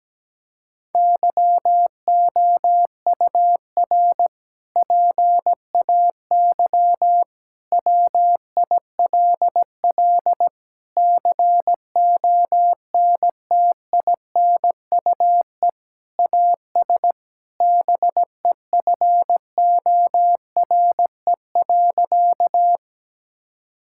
Morse code, heard as a keyed tone, about 17 wpm, 700 Hz. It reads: YOUR PAY WILL CONTINUE AS BEFORE.
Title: Sentences from Top 1000 Words - Review Part 6 - 17wpm